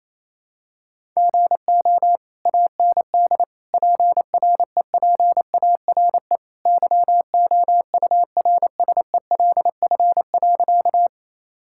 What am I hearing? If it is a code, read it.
Morse code, heard as a keyed tone, about 27 wpm, 700 Hz. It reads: GO AND PREPARE YOURSELF.
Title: Sentences from Top 1000 Words - Review Part 6 - 28wpm